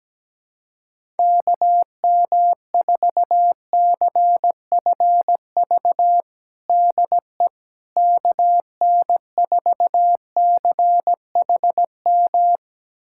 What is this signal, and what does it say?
Morse code, heard as a keyed tone, about 17 wpm, 700 Hz. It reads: KM4CFV DE KN4CHM
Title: USA Callsign DE Callsign - 6 Characters - 17wpm